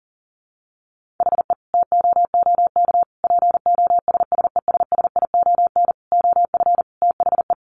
Morse code, heard as a keyed tone, about 40 wpm, 700 Hz. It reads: HE TOOK POSSESSION OF THE
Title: Sentences from Top 1500 Words 40wpm